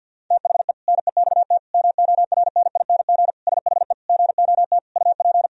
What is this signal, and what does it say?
Morse code, heard as a keyed tone, about 52 wpm, 700 Hz. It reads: THE NEXT MORNING SHE GOT UP
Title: Sentences from Top 1000 Words - Review Part 2 - 50wpm